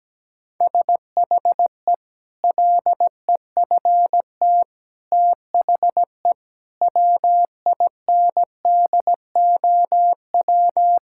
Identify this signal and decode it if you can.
Morse code, heard as a keyed tone, about 17 wpm, 700 Hz. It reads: SHE LEFT THE WINDOW
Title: Sentences from Top 1600 Words 17wpm